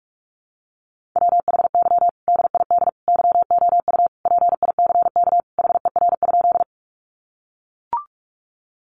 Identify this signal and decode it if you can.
Morse code, heard as a keyed tone, about 44 wpm, 700 Hz. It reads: WHY DID YOU PICK HER?
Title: Sentences from Top 1000 Words 45wpm